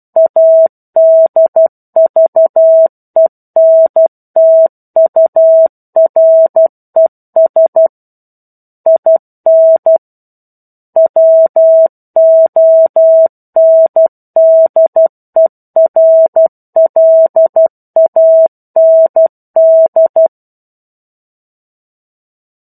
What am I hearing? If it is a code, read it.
Morse code, heard as a keyed tone, about 12 wpm, 650 Hz. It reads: ADVENTURES IN WONDERLAND
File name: 01-Chapter-01 12 WPM